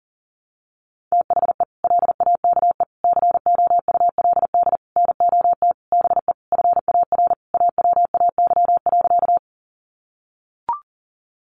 Morse code, heard as a keyed tone, about 40 wpm, 700 Hz. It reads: THE LAKE COULD NOT BE FAR AWAY.
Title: Sentences from Top 1100 Words 40wpm